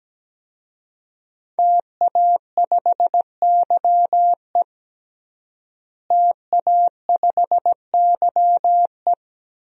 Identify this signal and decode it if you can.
Morse code, heard as a keyed tone, about 17 wpm, 700 Hz. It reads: TA5YE TA5YE
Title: DXCC-100 Call Signs - 5 Characters - Repeated 3x - 17wpm